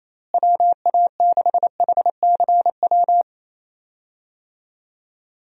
Morse code, heard as a keyed tone, about 27 wpm, 700 Hz. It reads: WA6HCW
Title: USA Call Signs - 6 Characters - Repeated 2x - 28wpm